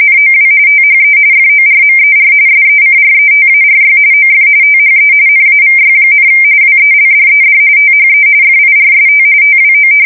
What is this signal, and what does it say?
RTTY, about 45.45 baud. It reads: NOW THE FORMER SECRETARY OF THE NAVY, WELDED THE FIRST TWO
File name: fsk170 / battleshipnj